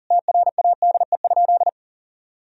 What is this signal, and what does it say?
Morse code, heard as a keyed tone, about 40 wpm, 700 Hz. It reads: TRADE?
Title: Sentences from Top 1000 Words - Review Part 4 - 40wpm